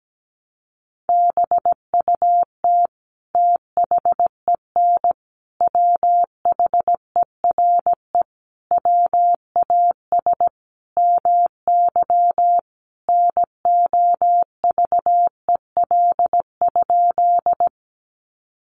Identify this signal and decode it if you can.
Morse code, heard as a keyed tone, about 17 wpm, 700 Hz. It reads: BUT THEN WHERE WAS MY NOVEL?
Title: Sentences from Top 1300 Words 17wpm